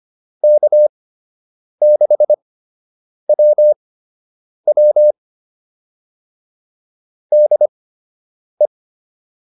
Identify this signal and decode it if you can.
Morse code, heard as a keyed tone, about 25 wpm, spaced as for 8 wpm, 600 Hz. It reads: K6WW DE